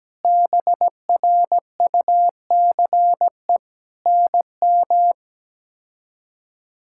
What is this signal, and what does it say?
Morse code, heard as a keyed tone, about 17 wpm, 700 Hz. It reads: BRUCE NM
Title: K1USN SST QSO Practice - 17wpm